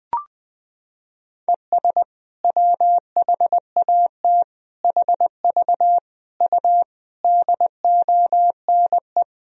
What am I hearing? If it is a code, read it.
Morse code, heard as a keyed tone, about 20 wpm, 700 Hz. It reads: ES WHAT HV U DONE
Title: Sentences from Top 500 Words and Abbreviations 20wpm